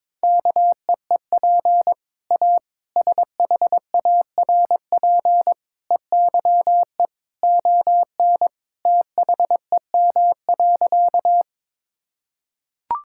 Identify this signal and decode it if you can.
Morse code, heard as a keyed tone, about 22 wpm, 700 Hz. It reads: KEEP A SHARP EYE ON THEM.
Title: Sentences from Top 1600 Words 22wpm